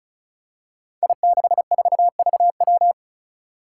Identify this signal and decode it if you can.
Morse code, heard as a keyed tone, about 34 wpm, 700 Hz. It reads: I64VW
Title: Five Challenging Letter-Number Combinations 35wpm